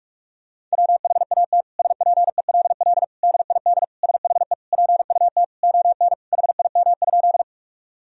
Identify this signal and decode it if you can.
Morse code, heard as a keyed tone, about 46 wpm, 700 Hz. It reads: WHAT SPELL DID SHE PUT ON HIM?
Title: Sentences from Top 1000 Words - Review Part 1 - 45wpm